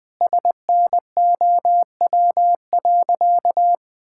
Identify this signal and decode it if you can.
Morse code, heard as a keyed tone, about 20 wpm, 700 Hz. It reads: SNOW.
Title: Sentences with 4-Word Phrases - 20wpm